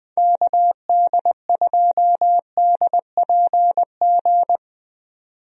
Morse code, heard as a keyed tone, about 21 wpm, 700 Hz. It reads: KD2DPG
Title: USA Call Signs - 6 Characters - 20wpm